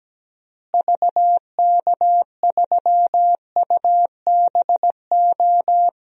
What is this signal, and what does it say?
Morse code, heard as a keyed tone, about 17 wpm, 700 Hz. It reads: VK3UBO
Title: DXCC-100 Call Signs - 6 Characters - Repeated 2x - 17wpm